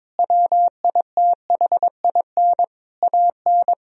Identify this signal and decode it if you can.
Morse code, heard as a keyed tone, about 22 wpm, 700 Hz. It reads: WITHIN AN
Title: Sentences from Top 1000 Words - Review Part 4 - 22wpm